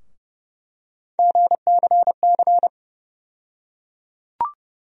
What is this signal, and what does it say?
Morse code, heard as a keyed tone, about 30 wpm, 700 Hz. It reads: GCC
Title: USA Call Sign Postfixes - 3 Letters 30wpm